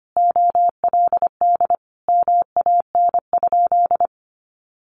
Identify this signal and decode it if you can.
Morse code, heard as a keyed tone, about 25 wpm, 700 Hz. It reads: OLD MAN?
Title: Sentences from Top 1100 Words 25wpm